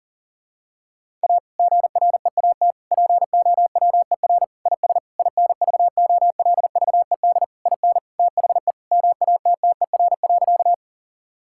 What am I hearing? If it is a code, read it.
Morse code, heard as a keyed tone, about 40 wpm, 700 Hz. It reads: A GREAT POWER IS INVOLVED IN THE MATTER.
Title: Sentences from Top 1000 Words - Review Part 5 - 40wpm